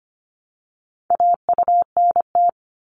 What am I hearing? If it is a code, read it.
Morse code, heard as a keyed tone, about 25 wpm, 700 Hz. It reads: AUNT
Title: Multi-letter Phonemes Repeated 3x 25wpm